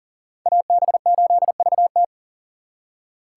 Morse code, heard as a keyed tone, about 51 wpm, 700 Hz. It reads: AB8VT